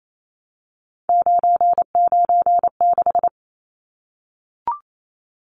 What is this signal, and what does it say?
Morse code, heard as a keyed tone, about 28 wpm, 700 Hz. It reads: 996
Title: Numbers - 3-Digit 28wpm